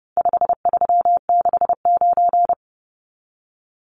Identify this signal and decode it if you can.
Morse code, heard as a keyed tone, about 30 wpm, 700 Hz. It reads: 5369
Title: Numbers - 4-Digit 30wpm